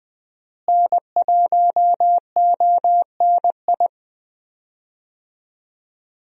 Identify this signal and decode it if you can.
Morse code, heard as a keyed tone, about 20 wpm, 700 Hz. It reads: N1ONI
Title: USA Call Signs - 5 Characters - Repeated 3x - 20wpm